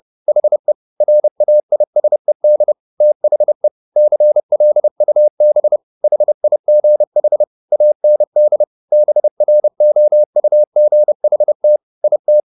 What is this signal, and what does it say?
Morse code, heard as a keyed tone, about 30 wpm, 600 Hz. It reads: HE RAISED THE CLUB HIGH AND BROUGHT IT